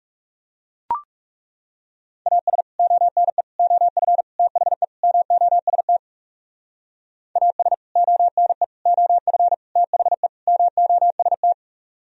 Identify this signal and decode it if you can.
Morse code, heard as a keyed tone, about 37 wpm, 700 Hz. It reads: AS ONE OF THE MOST AS ONE OF THE MOST
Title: Speed-Racing - Sets of 5 Words 30wpm